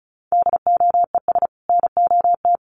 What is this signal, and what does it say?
Morse code, heard as a keyed tone, about 34 wpm, 700 Hz. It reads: DOES NOT
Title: Sets of 2 Words 35wpm